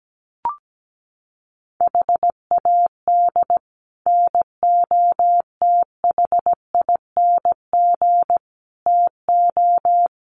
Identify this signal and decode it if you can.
Morse code, heard as a keyed tone, about 17 wpm, 700 Hz. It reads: HAD NOTHING TO
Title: Sets of 4 Words 17wpm